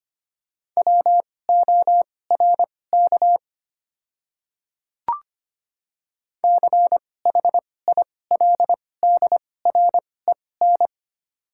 Morse code, heard as a keyed tone, about 25 wpm, spaced as for 18 wpm, 700 Hz. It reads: WORK CHILDREN
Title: 2x Character Spacing - Top 500 Words - 25wpm